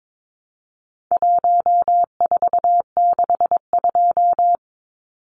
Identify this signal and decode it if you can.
Morse code, heard as a keyed tone, about 22 wpm, 700 Hz. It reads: 1462
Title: Numbers - 4-Digit 22wpm